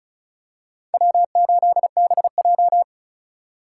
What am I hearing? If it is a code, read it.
Morse code, heard as a keyed tone, about 36 wpm, 700 Hz. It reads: W8BJ
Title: USA Call Signs - 4 Characters - 4x Character Spacing - 35wpm